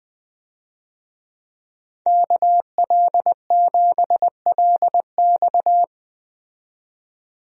Morse code, heard as a keyed tone, about 21 wpm, 700 Hz. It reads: KL7LX
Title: DXCC-100 Call Signs - 5 Characters - 20wpm